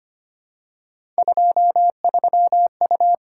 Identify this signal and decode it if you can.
Morse code, heard as a keyed tone, about 25 wpm, 700 Hz. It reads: 23U
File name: Three Letter-Number Combinations - 25wpm